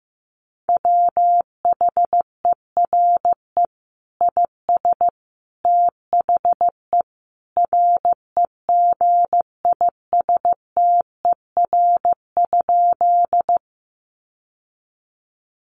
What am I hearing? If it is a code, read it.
Morse code, heard as a keyed tone, about 15 wpm, 700 Hz. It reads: WHERE IS THE REGISTER?
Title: Sentences from Top 1200 Words 15wpm